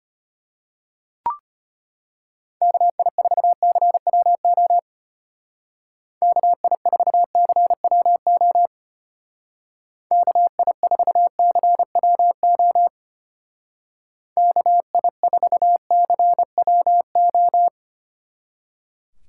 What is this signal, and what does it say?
Morse code, heard as a keyed tone, about 32 wpm, 700 Hz. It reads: KI4CWO KI4CWO KI4CWO KI4CWO